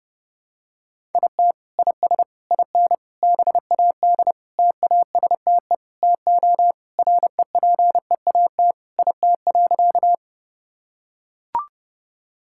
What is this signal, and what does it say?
Morse code, heard as a keyed tone, about 30 wpm, 700 Hz. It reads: IT IS IN BAD TASTE TO REPEAT IT.
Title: Sentences from Top 1600 Words 30wpm